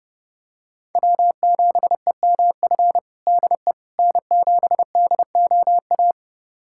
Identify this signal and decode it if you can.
Morse code, heard as a keyed tone, about 30 wpm, 700 Hz. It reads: W7EMF DE N7DOA
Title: USA Callsign DE Callsign - 5 Characters - 30wpm